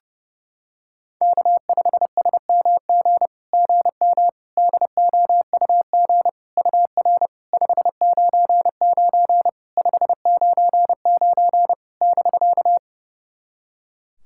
Morse code, heard as a keyed tone, about 30 wpm, 700 Hz. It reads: K5SMG GM DOUG UR 599 599 <BK>
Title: SOTA QSO Practice 30wpm